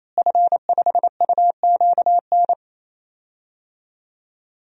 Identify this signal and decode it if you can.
Morse code, heard as a keyed tone, about 28 wpm, 700 Hz. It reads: F5UQN
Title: DXCC-100 Call Signs - 5 Characters - Repeated 3x - 28wpm